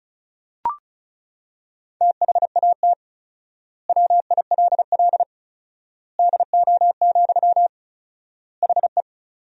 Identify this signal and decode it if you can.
Morse code, heard as a keyed tone, about 36 wpm, 700 Hz. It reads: THAT WILL DO, HE
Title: 4x Word-Spacing - Sentences from Top 100 Words - 35wpm